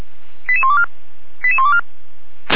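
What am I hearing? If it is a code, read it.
Five-tone selective call (ZVEI), heard as quick runs of five tones, tones of about 70 ms. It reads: 88124 88124
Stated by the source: Five-tone ZVEI selective call received on a software-defined radio.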